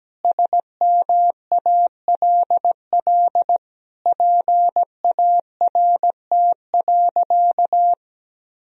Morse code, heard as a keyed tone, about 17 wpm, 700 Hz. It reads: SMALL PART.
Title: Sentences from Top 1000 Words - Review Part 1 - 17wpm